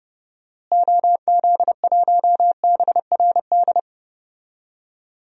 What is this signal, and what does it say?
Morse code, heard as a keyed tone, about 30 wpm, 700 Hz. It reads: OZ1BRD
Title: DXCC-100 Call Signs - 6 Characters - Repeated 3x - 30wpm